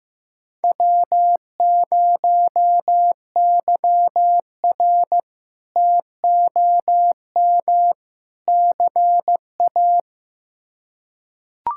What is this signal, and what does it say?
Morse code, heard as a keyed tone, about 15 wpm, 700 Hz. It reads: W0YR TOM CA
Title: K1USN SST QSO Practice - 15wpm